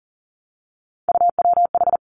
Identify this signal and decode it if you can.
Morse code, heard as a keyed tone, about 39 wpm, 700 Hz. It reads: UWH
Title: Three Challenging Letter-Number Combinations 40wpm